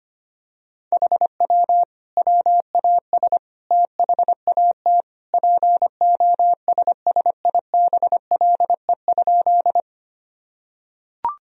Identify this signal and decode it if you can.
Morse code, heard as a keyed tone, about 25 wpm, 700 Hz. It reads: HW WAS THAT POSSIBLE?
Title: Sentences from Top 500 Words and Abbreviations 25wpm